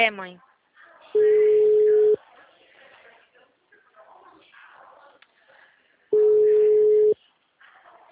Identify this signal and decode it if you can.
Ringback tone (Europe, 425 Hz), with speech.
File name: out- -2009- - - .40435